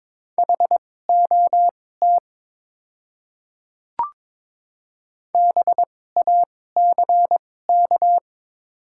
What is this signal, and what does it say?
Morse code, heard as a keyed tone, about 22 wpm, spaced as for 16 wpm, 700 Hz. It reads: HOT BACK
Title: 2x Character Spacing - Top 300 Words - 22wpm